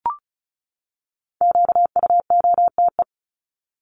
Morse code, heard as a keyed tone, about 34 wpm, 700 Hz. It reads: QUOTE
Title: Top 1751-2000 Words 35wpm